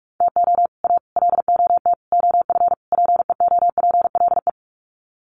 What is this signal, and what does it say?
Morse code, heard as a keyed tone, about 44 wpm, 700 Hz. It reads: TO A LOT OF PEOPLE
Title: Sets of 5 Words Encore 45wpm